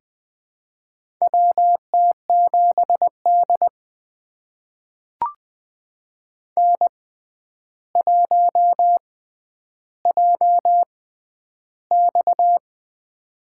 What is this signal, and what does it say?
Morse code, heard as a keyed tone, about 20 wpm, spaced as for 10 wpm, 700 Hz. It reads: WT7D N1JX
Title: USA Call Signs - 4 Characters - 6x Character Spacing - 20wpm